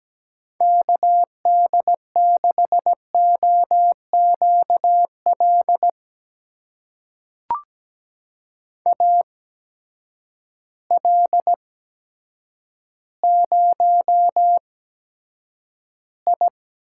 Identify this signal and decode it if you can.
Morse code, heard as a keyed tone, about 17 wpm, spaced as for 8 wpm, 700 Hz. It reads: KD6OQL AL0I
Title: USA Call Signs - 8x Character Spacing - 17wpm